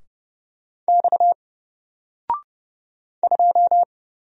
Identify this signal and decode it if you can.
Morse code, heard as a keyed tone, about 30 wpm, 700 Hz. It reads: X 2
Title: Single Character (Letter:Number:Punctuation:Prosign) 30wpm